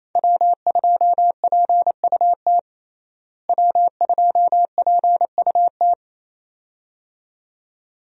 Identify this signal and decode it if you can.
Morse code, heard as a keyed tone, about 28 wpm, 700 Hz. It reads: W2PUT W2PUT
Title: USA Call Signs - 5 Characters - Repeated 2x - 28wpm